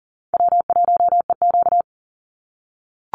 Morse code, heard as a keyed tone, about 40 wpm, 700 Hz. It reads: W1EQ
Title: Easy Call Signs - 40wpm